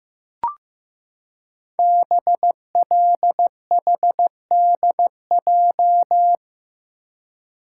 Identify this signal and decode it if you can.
Morse code, heard as a keyed tone, about 15 wpm, 700 Hz. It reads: BLHDJ
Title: Code Groups - 5 Letters - 15wpm